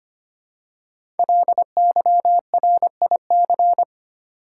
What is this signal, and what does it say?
Morse code, heard as a keyed tone, about 25 wpm, 700 Hz. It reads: LYRIC
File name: 5 Letter Words - First 3 Letters Given - 25wpm